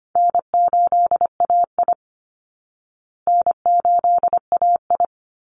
Morse code, heard as a keyed tone, about 25 wpm, 700 Hz. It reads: N8AI N8AI
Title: Call Signs Repeated 2x - 25wpm